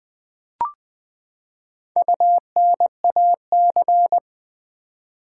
Morse code, heard as a keyed tone, about 19 wpm, 700 Hz. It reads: UNAC
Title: Build A Word - 4 Letters At A Time - 12-Letter Words - 20wpm